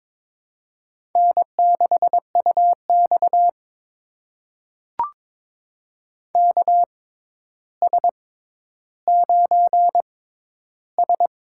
Morse code, heard as a keyed tone, about 22 wpm, spaced as for 11 wpm, 700 Hz. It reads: N6UX KS9S
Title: USA Call Signs - 4 Characters - 6x Character Spacing - 22wpm